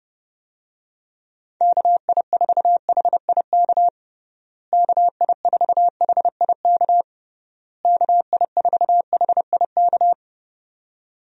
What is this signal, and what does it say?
Morse code, heard as a keyed tone, about 30 wpm, 700 Hz. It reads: KI4HIK KI4HIK KI4HIK
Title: USA Call Signs - 6 Characters - Repeated 3x - 30wpm